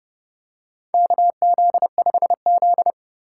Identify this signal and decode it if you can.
Morse code, heard as a keyed tone, about 32 wpm, 700 Hz. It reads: KZ5Z
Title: K1USN SST QSO Practice - 30wpm